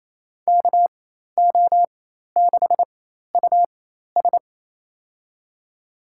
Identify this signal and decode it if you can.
Morse code, heard as a keyed tone, about 28 wpm, spaced as for 13 wpm, 700 Hz. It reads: KO6US
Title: USA Call Signs - 5 Characters - 4x Character Spacing - 28wpm